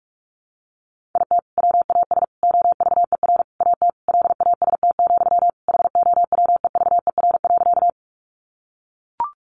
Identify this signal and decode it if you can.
Morse code, heard as a keyed tone, about 44 wpm, 700 Hz. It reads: IT WAS OVER AT LAST, HOWEVER.
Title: Sentences from Top 600 Words 45wpm